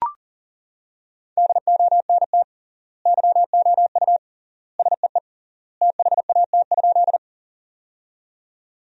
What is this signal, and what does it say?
Morse code, heard as a keyed tone, about 37 wpm, 700 Hz. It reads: DONT YOU SEE THAT?
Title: 3x Word-Spacing - Sentences from Top 300 Words - 40wpm